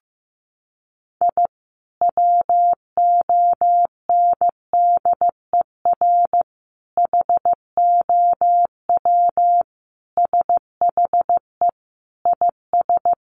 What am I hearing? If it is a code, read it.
Morse code, heard as a keyed tone, about 15 wpm, 700 Hz. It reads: I WONDER HOW SHE IS
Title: Sentences from Top 1500 Words 15wpm